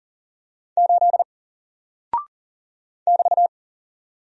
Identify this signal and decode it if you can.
Morse code, heard as a keyed tone, about 40 wpm, 700 Hz. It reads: 8 <BT>
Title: Single Character (Letter:Number:Punctuation:Prosign) Alt 40wpm